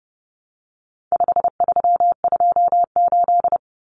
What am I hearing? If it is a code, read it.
Morse code, heard as a keyed tone, about 30 wpm, 700 Hz. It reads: 5328
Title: Numbers - 4-Digit 30wpm